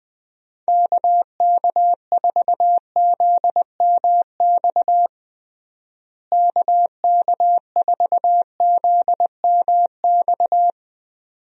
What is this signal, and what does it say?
Morse code, heard as a keyed tone, about 20 wpm, 700 Hz. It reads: KK4ZMX KK4ZMX
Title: USA Call Signs - 6 Characters - Repeated 2x - 20wpm